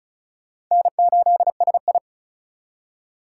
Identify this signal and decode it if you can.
Morse code, heard as a keyed tone, about 48 wpm, spaced as for 42 wpm, 700 Hz. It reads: N8SI
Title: Speed Racing - USA 1x2 Call Signs 35wpm